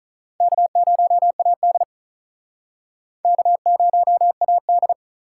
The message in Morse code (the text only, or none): K0AD K0AD